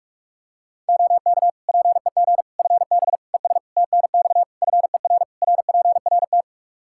Morse code, heard as a keyed tone, about 41 wpm, 700 Hz. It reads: OK PEG FB ES TNX FER RPRT
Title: Practice QSO - Exchange 2 - Info - 45wpm